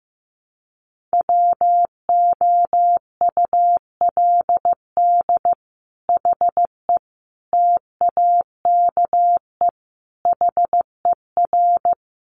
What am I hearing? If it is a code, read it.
Morse code, heard as a keyed tone, about 15 wpm, 700 Hz. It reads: WOULD HE TAKE HER
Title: Sentences from Top 1500 Words 15wpm